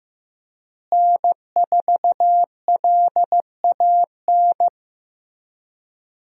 Morse code, heard as a keyed tone, about 15 wpm, 700 Hz. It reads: N4LAN
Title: USA Call Signs - 5 Characters - Repeated 2x - 15wpm